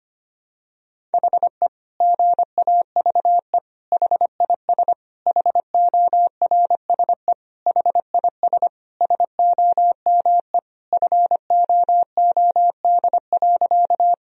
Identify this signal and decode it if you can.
Morse code, heard as a keyed tone, about 25 wpm, 700 Hz. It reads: HE GAVE HIS HORSE HIS SOME FOOD.
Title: Sentences from Top 1000 Words - Review Part 1 - 25wpm